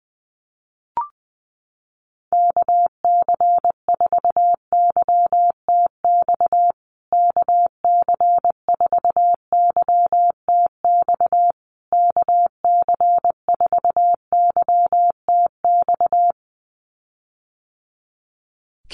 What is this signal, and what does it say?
Morse code, heard as a keyed tone, about 20 wpm, 700 Hz. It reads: KC4YTX KC4YTX KC4YTX
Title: USA Call Signs Repeated 3x 20wpm